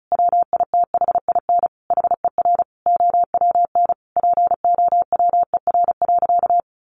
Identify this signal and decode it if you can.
Morse code, heard as a keyed tone, about 34 wpm, 700 Hz. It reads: WITHIN HER OWN POWER.
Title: Sentences from Top 1200 Words 35wpm